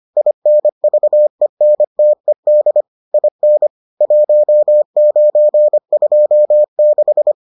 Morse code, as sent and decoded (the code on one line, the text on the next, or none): .. -. ...- . -. - . -.. / .. -. / .---- ----. ..--- -....
INVENTED IN 1926